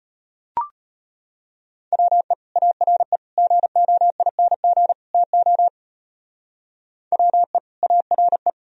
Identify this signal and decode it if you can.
Morse code, heard as a keyed tone, about 29 wpm, 700 Hz. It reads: WE ARE GOING TO WE ARE
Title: Speed-Racing - Sets of 4 Words 25wpm